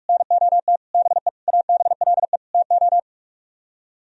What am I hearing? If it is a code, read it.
Morse code, heard as a keyed tone, about 46 wpm, 700 Hz. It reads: NOT BE ABLE TO